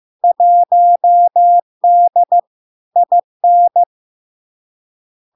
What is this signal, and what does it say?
Morse code, heard as a keyed tone, about 15 wpm, 700 Hz. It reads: 1D IN
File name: LARK-FD-2022-Log-in-CW-15wpm-700-Hz